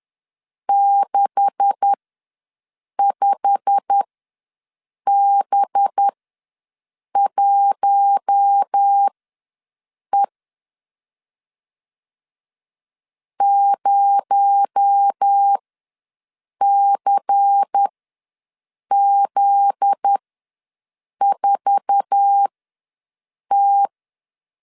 Morse code, heard as a keyed tone, about 10 wpm, spaced as for 6 wpm, 800 Hz. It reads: 65B1E 0CZ4T
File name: LEZIONI / 9 286 ws 10wpm 4032